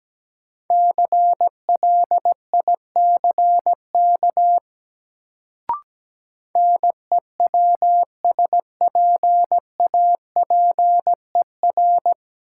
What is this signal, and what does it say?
Morse code, heard as a keyed tone, about 17 wpm, 700 Hz. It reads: CLICK NEWSPAPER
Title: Sentences from Top 1100 Words - New Word Drill Alt - 17wpm